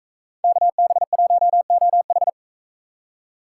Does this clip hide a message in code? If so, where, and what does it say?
Yes: Morse code from 0.4 to 2.3 s, heard as a keyed tone, about 40 wpm, 700 Hz. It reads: KB1OH